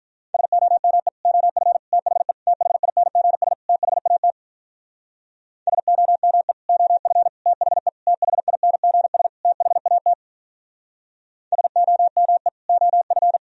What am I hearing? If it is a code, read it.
Morse code, heard as a keyed tone, about 47 wpm, 700 Hz. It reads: SOME OF THE THINGS THAT SOME OF THE THINGS THAT SOME OF